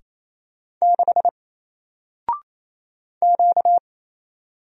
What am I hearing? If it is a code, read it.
Morse code, heard as a keyed tone, about 27 wpm, 700 Hz. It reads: 6 Q
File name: Single-Letter-Number-28wpm